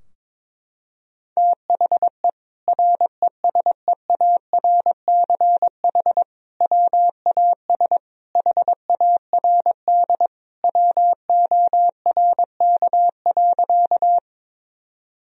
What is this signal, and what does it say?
Morse code, heard as a keyed tone, about 22 wpm, 700 Hz. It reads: THE RESEARCH WAS HARD WORK.
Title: Sentences from Top 1000 Words - Review Part 4 - 22wpm